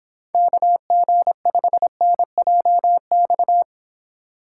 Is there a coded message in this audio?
Yes: Morse code from 0.3 to 3.6 s, heard as a keyed tone, about 26 wpm, 700 Hz. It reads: KG5NJX